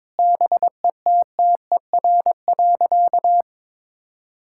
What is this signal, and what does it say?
Morse code, heard as a keyed tone, about 22 wpm, 700 Hz. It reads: BETTER.
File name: practice / 22wpm / Sentences with 2-Word Phrases - 22wpm